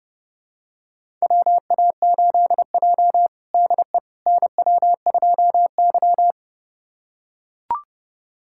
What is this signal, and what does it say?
Morse code, heard as a keyed tone, about 30 wpm, 700 Hz. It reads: WA8J DE NW2Y
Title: USA Callsign DE Callsign - 4 Characters - 30wpm